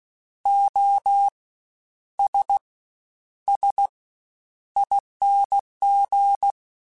Morse code, heard as a keyed tone, about 16 wpm, spaced as for 9 wpm, 800 Hz. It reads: OSSING
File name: Special INGs-mp3s / Special INGs-mp3s / Fast ing @15